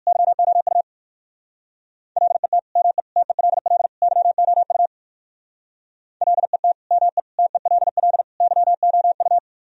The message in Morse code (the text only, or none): YOU LET ME TELL YOU LET ME TELL YOU